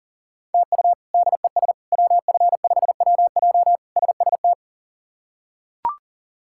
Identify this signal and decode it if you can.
Morse code, heard as a keyed tone, about 40 wpm, 700 Hz. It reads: TU DES WF5WJ SST
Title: K1USN SST QSO Practice - 40wpm